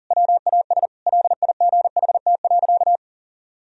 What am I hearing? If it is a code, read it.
Morse code, heard as a keyed tone, about 40 wpm, 700 Hz. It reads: WAS LIGHT.
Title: Sentences from Top 1000 Words - Review Part 1 - 40wpm